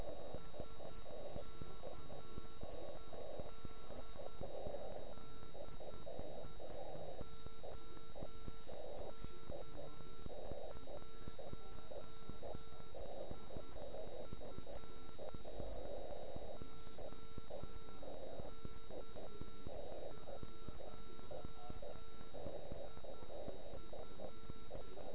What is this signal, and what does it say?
Morse code, heard as a keyed tone, about 9 wpm, 1340 Hz. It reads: SKED DE OK1IF OK1IF